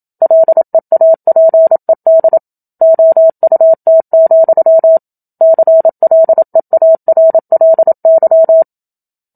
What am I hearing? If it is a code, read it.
Morse code, heard as a keyed tone, about 27 wpm, 650 Hz. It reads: LEAPED OUT, CLEARLY
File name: Book Two / 01-Chapter 1 - 27 WPM